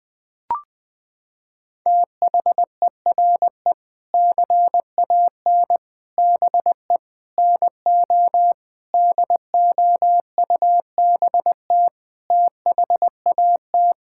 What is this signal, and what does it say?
Morse code, heard as a keyed tone, about 21 wpm, 700 Hz. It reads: THERE CAN BE NO DOUBT THAT
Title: Sentences with 7-Word Phrases - 20wpm